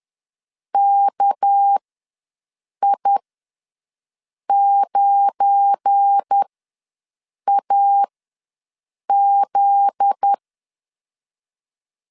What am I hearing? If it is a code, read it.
Morse code, heard as a keyed tone, about 10 wpm, spaced as for 6 wpm, 800 Hz. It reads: KI9AZ
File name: LEZIONI / 7 GF9Z / 7 GF9Z ws 10wpm 3919